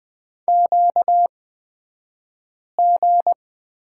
Morse code, heard as a keyed tone, about 20 wpm, spaced as for 5 wpm, 700 Hz. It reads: QG
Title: 235 Lesson 38d - Letter q - Two Characters 20wpm